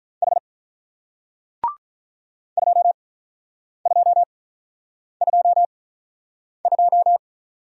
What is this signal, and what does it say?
Morse code, heard as a keyed tone, about 50 wpm, spaced as for 9 wpm, 700 Hz. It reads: H 2222